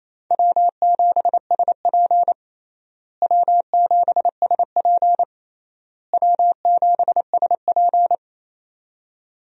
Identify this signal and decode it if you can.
Morse code, heard as a keyed tone, about 28 wpm, 700 Hz. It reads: W7SP W7SP W7SP
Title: USA Call Signs - 4 Characters - Repeated 3x - 28wpm